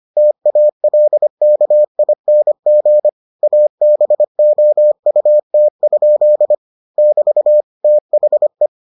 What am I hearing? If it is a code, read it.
Morse code, heard as a keyed tone, about 25 wpm, 600 Hz. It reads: TALKING ABOUT? <BT> THE